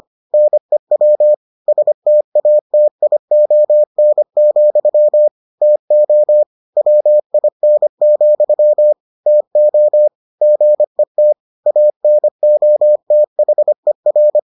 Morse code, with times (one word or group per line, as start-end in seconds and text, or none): 0.3-1.3 NEW
1.7-5.3 STATION,
5.6-6.4 TO
6.8-8.9 WIN,
9.3-10.1 TO
10.4-11.3 GET
11.7-14.4 ANOTHER